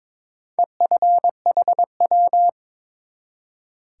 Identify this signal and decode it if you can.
Morse code, heard as a keyed tone, about 22 wpm, 700 Hz. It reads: EFHW
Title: Top 180 QSO Elements 22wpm